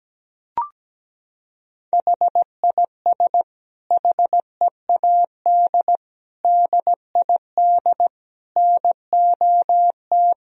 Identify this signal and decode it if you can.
Morse code, heard as a keyed tone, about 17 wpm, 700 Hz. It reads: HIS HEAD DID NOT
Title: Sentences from Top 1000 Words - Review Part 2 - 17wpm